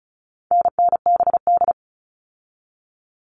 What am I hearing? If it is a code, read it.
Morse code, heard as a keyed tone, about 34 wpm, 700 Hz. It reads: NNBD
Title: All Cut Numbers - 4-Digit 35wpm